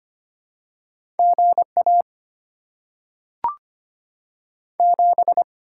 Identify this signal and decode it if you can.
Morse code, heard as a keyed tone, about 25 wpm, 700 Hz. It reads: GA 7
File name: Top 20 QSO Elements - 6x Character Spacing - 25wpm